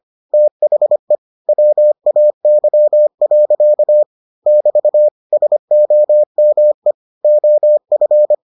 Morse code, heard as a keyed tone, about 25 wpm, 600 Hz. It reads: THE WAY. <BT> SOME OF